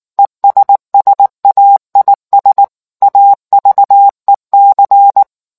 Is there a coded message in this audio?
Yes: Morse code from 0.2 to 5.2 s, heard as a keyed tone, about 19 wpm, 800 Hz. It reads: ESSAIS AVEC